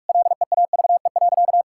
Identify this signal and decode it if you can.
Morse code, heard as a keyed tone, about 46 wpm, 700 Hz. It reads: LEAVE.
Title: Sentences from Top 1000 Words - Review Part 5 - 45wpm